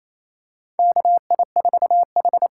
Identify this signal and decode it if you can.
Morse code, heard as a keyed tone, about 28 wpm, 700 Hz. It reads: KI4H
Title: USA Call Signs - 6 Characters - 8x Character Spacing - 28wpm